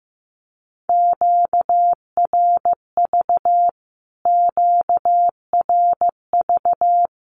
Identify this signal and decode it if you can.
Morse code, heard as a keyed tone, about 15 wpm, 700 Hz. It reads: QRV QRV
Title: Q-Codes Repeated 3x 15wpm